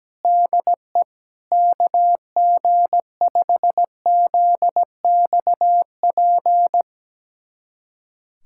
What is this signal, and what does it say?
Morse code, heard as a keyed tone, about 17 wpm, 700 Hz. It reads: DE KG5ZXP
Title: USA Callsign DE Callsign - 6 Characters - 17wpm